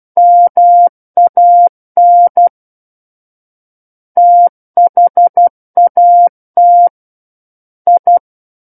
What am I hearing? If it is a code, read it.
Morse code, heard as a keyed tone, about 12 wpm, 700 Hz. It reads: MAN THAT I
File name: Book One / 03-Chapter 3 - 12 WPM